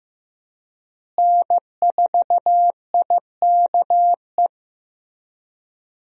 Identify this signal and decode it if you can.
Morse code, heard as a keyed tone, about 15 wpm, 700 Hz. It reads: N4IKE
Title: USA Call Signs - 5 Characters - Repeated 3x - 15wpm